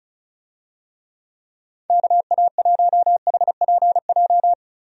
Morse code, heard as a keyed tone, about 36 wpm, 700 Hz. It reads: KA1HPJ
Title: USA Call Signs - 6 Characters - 35wpm